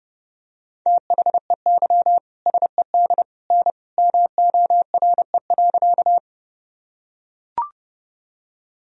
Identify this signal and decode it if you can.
Morse code, heard as a keyed tone, about 30 wpm, 700 Hz. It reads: THEY SED N MORE.
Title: Sentences from Top 100 Words and Abbreviations 30wpm